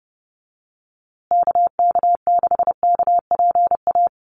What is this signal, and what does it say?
Morse code, heard as a keyed tone, about 30 wpm, 700 Hz. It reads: KK6KPA
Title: USA Call Signs Repeated 3x 30wpm